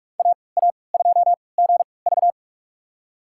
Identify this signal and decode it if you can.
Morse code, heard as a keyed tone, about 41 wpm, spaced as for 25 wpm, 700 Hz. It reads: AA2GV
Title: USA Call Signs - 5 Characters - 3x Character Spacing - 45wpm